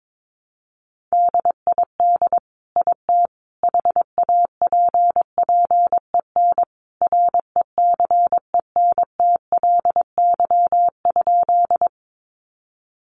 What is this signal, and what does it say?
Morse code, heard as a keyed tone, about 22 wpm, 700 Hz. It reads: DID IT HAPPEN RECENTLY?
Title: Sentences from Top 1200 Words 22wpm